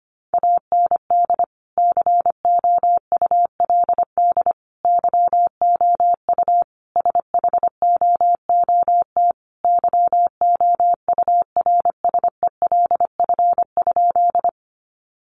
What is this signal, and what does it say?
Morse code, heard as a keyed tone, about 25 wpm, 700 Hz. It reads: AND COULD YOU SHOOT YOURSELF?
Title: Sentences from Top 1200 Words 25wpm